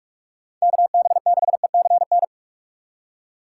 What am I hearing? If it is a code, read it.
Morse code, heard as a keyed tone, about 46 wpm, 700 Hz. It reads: KB6ECN
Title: USA Call Signs - 6 Characters - 2x Character Spacing - 45wpm